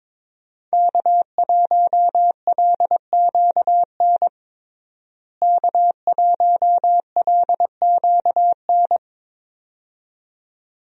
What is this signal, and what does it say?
Morse code, heard as a keyed tone, about 22 wpm, 700 Hz. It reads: K1LQN K1LQN
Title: USA Call Signs - 5 Characters - Repeated 2x - 22wpm